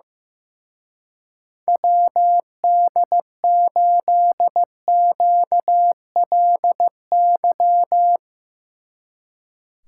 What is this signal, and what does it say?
Morse code, heard as a keyed tone, about 16 wpm, 700 Hz. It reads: WD8QLY
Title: Speed Racing - USA 2x3 Call Signs 15wpm